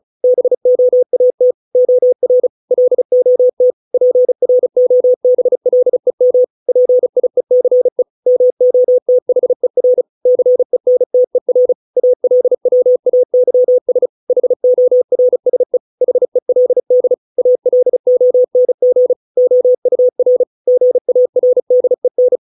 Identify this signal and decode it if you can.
Morse code, heard as a keyed tone, about 34 wpm, 500 Hz. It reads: BOAT OR LOT PROBLEM PIECE MOTHER CENTER ALWAYS HORSE HELD ALONG OUR GARDEN